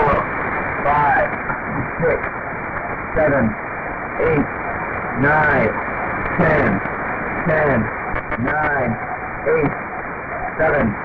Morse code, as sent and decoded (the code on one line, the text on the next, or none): none